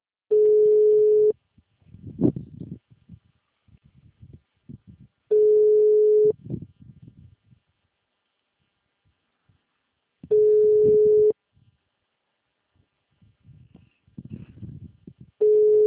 Ringback tone (Europe, 425 Hz).
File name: out- -2007- - - .47946